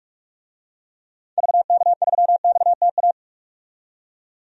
Morse code, heard as a keyed tone, about 46 wpm, 700 Hz. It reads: VK3XTA
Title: DXCC-100 Call Signs - 6 Characters - 45wpm